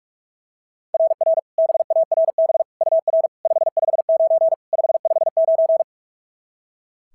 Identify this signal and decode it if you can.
Morse code, heard as a keyed tone, about 41 wpm, 660 Hz. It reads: RR BARB UR 559 559
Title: SOTA QSO Practice 45wpm